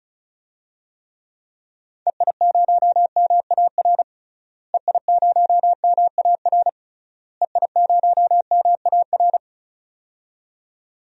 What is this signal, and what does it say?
Morse code, heard as a keyed tone, about 36 wpm, 700 Hz. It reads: EI0MAR EI0MAR EI0MAR
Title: DXCC-100 Call Signs - 6 Characters - Repeated 3x - 35wpm